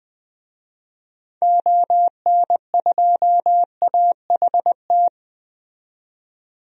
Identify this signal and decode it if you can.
Morse code, heard as a keyed tone, about 21 wpm, 700 Hz. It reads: ON2AHT
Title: DXCC-100 Call Signs - 6 Characters - 20wpm